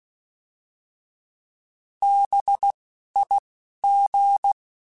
Morse code, heard as a keyed tone, about 16 wpm, spaced as for 11 wpm, 800 Hz. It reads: BIG